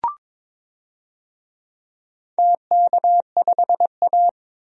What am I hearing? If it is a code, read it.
Morse code, heard as a keyed tone, about 22 wpm, 700 Hz. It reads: TK5A
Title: DXCC-100 Call Signs - 4 Characters - 22wpm